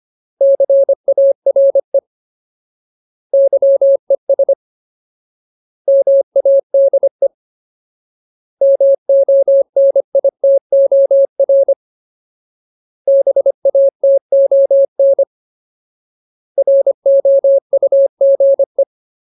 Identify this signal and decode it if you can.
Morse code, heard as a keyed tone, about 25 wpm, 550 Hz. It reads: CARE YES MADE MONITOR BATON ROUGE
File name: Ultimate Word List / 25WPM Ultimate